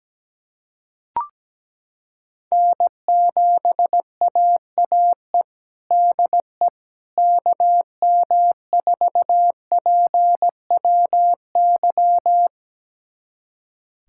Morse code, heard as a keyed tone, about 17 wpm, 700 Hz. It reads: N7AAE DE KM4PWY
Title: USA Callsign DE Callsign 17wpm